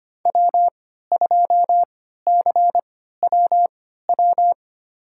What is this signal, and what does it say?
Morse code, heard as a keyed tone, about 25 wpm, spaced as for 14 wpm, 700 Hz. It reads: W2CWW
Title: USA Call Signs - 5 Characters - 3x Character Spacing - 25wpm